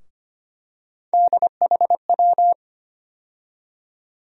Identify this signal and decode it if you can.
Morse code, heard as a keyed tone, about 25 wpm, 700 Hz. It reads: DHW
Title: USA Call Sign Postfixes - 3 Letters 25wpm